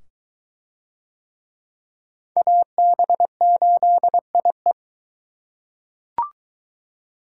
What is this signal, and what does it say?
Morse code, heard as a keyed tone, about 21 wpm, 700 Hz. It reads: AB8IE